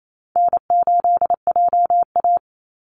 Morse code, heard as a keyed tone, about 28 wpm, 700 Hz. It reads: N8JA
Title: Easy Call Signs Repeated 3x - 28wpm